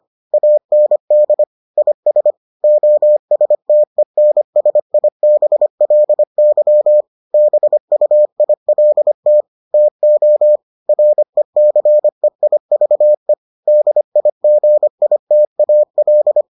Morse code, heard as a keyed tone, about 25 wpm, 600 Hz. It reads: AND IS OSTENSIBLY BUILT TO RECEIVE DIGITAL